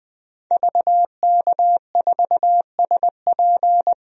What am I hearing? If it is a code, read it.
Morse code, heard as a keyed tone, about 20 wpm, 700 Hz. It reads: VK4SP
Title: DXCC-100 Call Signs - 5 Characters - 20wpm